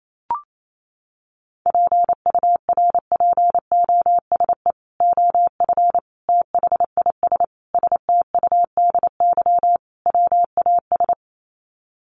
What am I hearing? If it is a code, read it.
Morse code, heard as a keyed tone, about 28 wpm, 700 Hz. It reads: PURPOSE OF THIS STUDY WAS
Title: Sets of 5 Words 28wpm